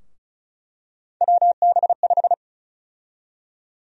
Morse code, heard as a keyed tone, about 36 wpm, 700 Hz. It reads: WB5
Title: USA Call Sign Prefixes - Repeated 2x - 35wpm